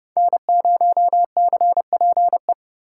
Morse code, heard as a keyed tone, about 28 wpm, 700 Hz. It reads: N0CPE